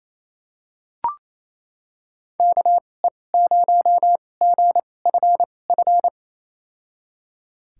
Morse code, heard as a keyed tone, about 28 wpm, spaced as for 20 wpm, 700 Hz. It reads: KE0GFF
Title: USA Call Signs - 2x Character Spacing - 28wpm